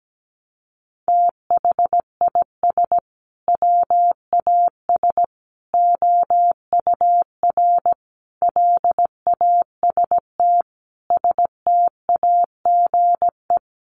Morse code, heard as a keyed tone, about 17 wpm, 700 Hz. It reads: THIS WAS OUR LAST STAGE
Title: Sentences from Top 1200 Words 17wpm